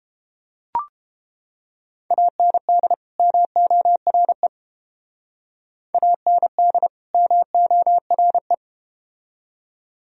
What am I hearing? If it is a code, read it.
Morse code, heard as a keyed tone, about 29 wpm, 700 Hz. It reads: AND MORE AND MORE